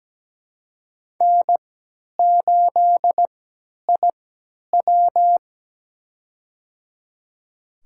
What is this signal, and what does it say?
Morse code, heard as a keyed tone, about 17 wpm, spaced as for 10 wpm, 700 Hz. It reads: N8IW
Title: USA Call Signs - 4 Characters - 3x Character Spacing - 17wpm